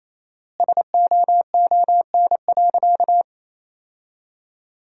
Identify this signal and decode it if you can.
Morse code, heard as a keyed tone, about 28 wpm, 700 Hz. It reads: SOON.
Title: 4x Word-Spacing - Sentences from Top 400 Words - 28wpm